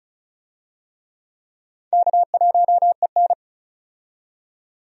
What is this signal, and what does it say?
Morse code, heard as a keyed tone, about 34 wpm, 700 Hz. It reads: K1EN
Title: USA Call Signs - 4 Characters - 35wpm